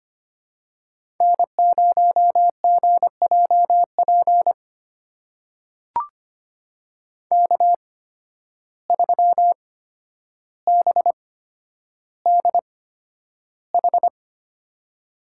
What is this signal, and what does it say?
Morse code, heard as a keyed tone, about 25 wpm, spaced as for 11 wpm, 700 Hz. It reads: N0GJP K3BDH
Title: USA Call Signs - 5 Characters - 8x Character Spacing - 25wpm